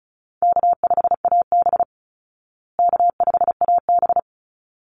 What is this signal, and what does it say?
Morse code, heard as a keyed tone, about 34 wpm, 700 Hz. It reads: K5AB K5AB
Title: Easy Call Signs Repeated 2x - 35wpm